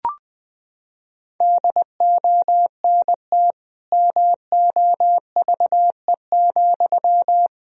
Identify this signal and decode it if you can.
Morse code, heard as a keyed tone, about 20 wpm, 700 Hz. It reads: DONT MOVE,
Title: Sentences from Top 300 Words and Abbreviations 20wpm